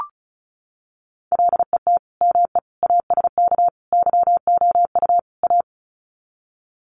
Morse code, heard as a keyed tone, about 34 wpm, 700 Hz. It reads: LET ME ASK YOU A
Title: Sets of 5 Words Encore 35wpm